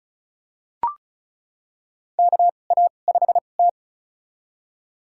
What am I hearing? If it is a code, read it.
Morse code, heard as a keyed tone, about 34 wpm, spaced as for 25 wpm, 700 Hz. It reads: KA5T
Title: USA Call Signs - 4 Characters - 2x Character Spacing - 35wpm